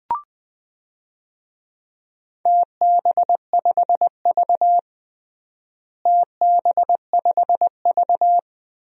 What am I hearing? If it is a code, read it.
Morse code, heard as a keyed tone, about 20 wpm, 700 Hz. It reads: TB5V TB5V
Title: DXCC-100 Call Signs - 4 Characters - Repeated 2x - 20wpm